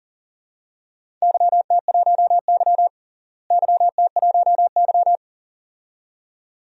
Morse code, heard as a keyed tone, about 40 wpm, 700 Hz. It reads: YT1Y YT1Y
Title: DXCC-100 Call Signs - 4 Characters - Repeated 2x - 40wpm